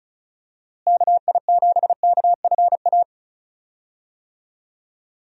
Morse code, heard as a keyed tone, about 36 wpm, 700 Hz. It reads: KI7KFA
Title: USA Call Signs - 6 Characters - Repeated 3x - 35wpm